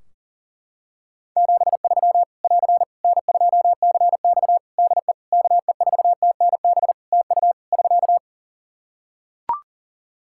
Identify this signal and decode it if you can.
Morse code, heard as a keyed tone, about 40 wpm, 700 Hz. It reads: 73 <AR> N2CX DE KE4TNB TU <SK>
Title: Practice QSO - Exchange 4 - Ending - 40wpm